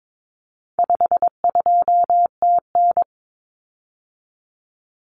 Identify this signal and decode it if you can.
Morse code, heard as a keyed tone, about 22 wpm, 700 Hz. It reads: 52TN
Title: Common Cut Numbers - 4-Digit 22wpm